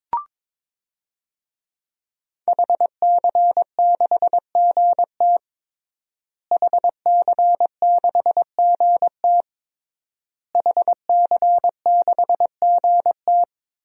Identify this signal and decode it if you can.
Morse code, heard as a keyed tone, about 22 wpm, 700 Hz. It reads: HC6GT HC6GT HC6GT
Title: DXCC-100 Call Signs - 5 Characters - Repeated 3x - 22wpm